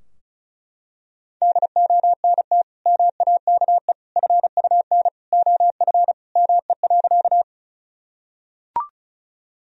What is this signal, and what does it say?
Morse code, heard as a keyed tone, about 36 wpm, 700 Hz. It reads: DONT MAKE FUN OF ME.
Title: Sentences from Top 1000 Words - Review Part 4 - 35wpm